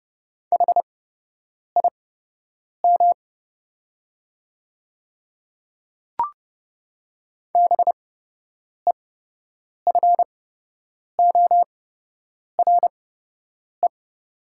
Morse code, heard as a keyed tone, about 30 wpm, spaced as for 8 wpm, 700 Hz. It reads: HIM BEFORE
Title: 8x Character Spacing - Top 500 Words - 30wpm